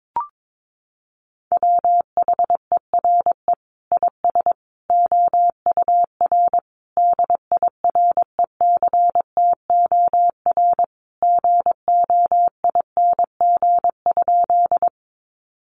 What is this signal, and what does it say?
Morse code, heard as a keyed tone, about 22 wpm, 700 Hz. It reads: WHERE IS OUR DIRECTOR GOING?
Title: Sentences from Top 1300 Words 22wpm